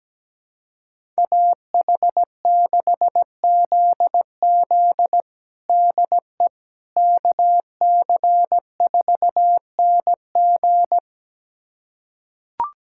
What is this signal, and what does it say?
Morse code, heard as a keyed tone, about 17 wpm, 700 Hz. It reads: AH6ZZ DE KC4NG
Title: USA Callsign DE Callsign - 5 Characters - 17wpm